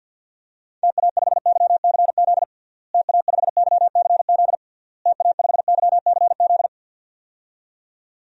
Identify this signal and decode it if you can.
Morse code, heard as a keyed tone, about 51 wpm, 700 Hz. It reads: TA5YCZ TA5YCZ TA5YCZ
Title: DXCC-100 Call Signs - 6 Characters - Repeated 3x - 50wpm